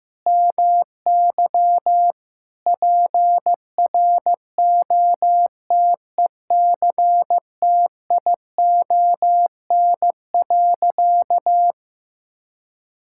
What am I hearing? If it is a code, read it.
Morse code, heard as a keyed tone, about 15 wpm, 700 Hz. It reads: MY PROTECTION.
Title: Sentences from Top 1600 Words 15wpm